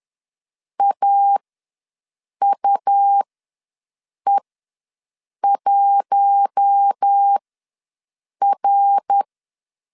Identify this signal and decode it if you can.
Morse code, heard as a keyed tone, about 10 wpm, spaced as for 6 wpm, 800 Hz. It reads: AUE1R
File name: LEZIONI / 5 LBVC / 5 LBVC ws 10wpm 3733